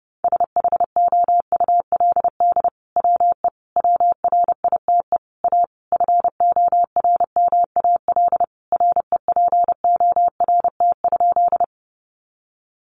Morse code, heard as a keyed tone, about 28 wpm, 700 Hz. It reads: SHOULD WE WRITE A FORMAL REPORT?
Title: Sentences from Top 1300 Words 30wpm